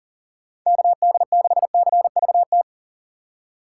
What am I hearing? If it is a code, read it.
Morse code, heard as a keyed tone, about 40 wpm, 700 Hz. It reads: KD6CVT
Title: USA Call Signs - 6 Characters - 6x Character Spacing - 40wpm